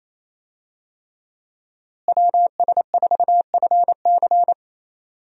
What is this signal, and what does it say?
Morse code, heard as a keyed tone, about 28 wpm, 700 Hz. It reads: WS4FC
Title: USA Call Signs - 5 Characters - Repeated 3x - 28wpm